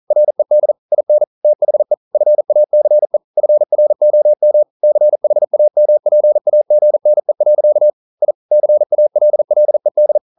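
Morse code, heard as a keyed tone, about 42 wpm, 600 Hz. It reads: RED IN THE FACE FROM CHAMPAGNE. I CALLED
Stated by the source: Morse code practice at 41 wpm, from CW med Gnister.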